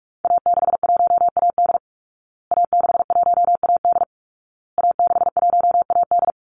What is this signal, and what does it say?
Morse code, heard as a keyed tone, about 44 wpm, 700 Hz. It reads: A61AD A61AD A61AD
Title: Call Signs Repeated 2x - 45wpm